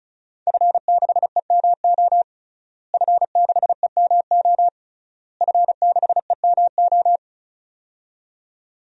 Morse code, heard as a keyed tone, about 36 wpm, 700 Hz. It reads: F6EMO F6EMO F6EMO
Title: DXCC-100 Call Signs - 5 Characters - Repeated 3x - 35wpm